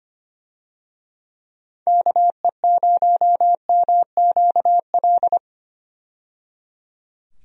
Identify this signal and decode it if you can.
Morse code, heard as a keyed tone, about 25 wpm, 700 Hz. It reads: KE0MQL
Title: USA Call Signs - 6 Characters - 25wpm